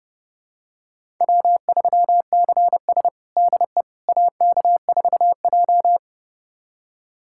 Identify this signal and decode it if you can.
Morse code, heard as a keyed tone, about 30 wpm, 700 Hz. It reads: W3CS DE AK4J
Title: USA Callsign DE Callsign - 4 Characters - 30wpm